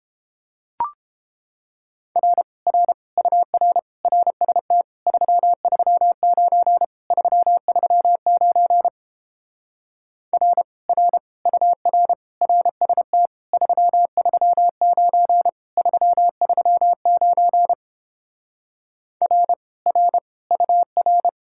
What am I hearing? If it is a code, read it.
Morse code, heard as a keyed tone, about 29 wpm, 700 Hz. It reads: R R UR RST 339 339 R R UR RST 339 339 R R UR